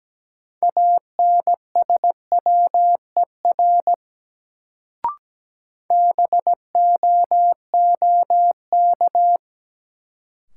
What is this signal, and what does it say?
Morse code, heard as a keyed tone, about 17 wpm, 700 Hz. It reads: ANSWER BOOK
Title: Sentences from Top 300 Words - New Word Drill - 17wpm